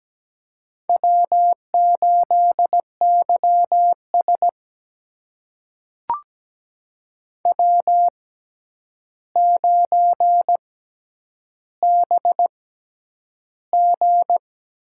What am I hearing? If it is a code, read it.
Morse code, heard as a keyed tone, about 17 wpm, spaced as for 9 wpm, 700 Hz. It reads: W8YS W9BG
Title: USA Call Signs - 4 Characters - 6x Character Spacing - 17wpm